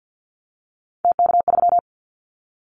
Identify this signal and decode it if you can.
Morse code, heard as a keyed tone, about 50 wpm, 700 Hz. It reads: TK3
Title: Callsign Prefixes for Top 100 DXCC - 50wpm